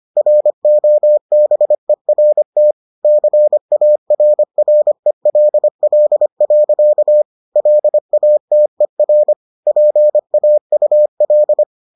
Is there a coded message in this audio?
Yes: Morse code from 0.2 to 11.6 s, heard as a keyed tone, about 25 wpm, 600 Hz. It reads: ROBERT CARRELL. LATER PAUL